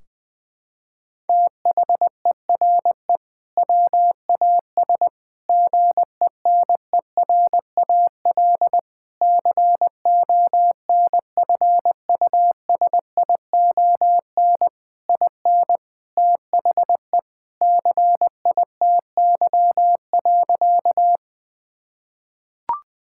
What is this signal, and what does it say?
Morse code, heard as a keyed tone, about 20 wpm, 700 Hz. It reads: THERE WAS GENERAL CONFUSION IN THE CITY.
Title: Sentences from Top 1600 Words 20wpm